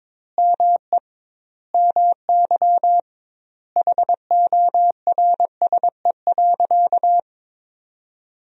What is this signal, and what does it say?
Morse code, heard as a keyed tone, about 22 wpm, 700 Hz. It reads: ME MY HORSE.
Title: 2x Word-Spacing - Sentences from Top 500 Words - 22wpm